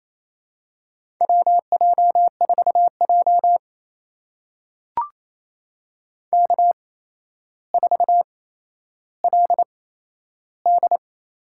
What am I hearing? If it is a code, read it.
Morse code, heard as a keyed tone, about 28 wpm, spaced as for 12 wpm, 700 Hz. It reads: WJ4J K4LD
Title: USA Call Signs - 4 Characters - 8x Character Spacing - 28wpm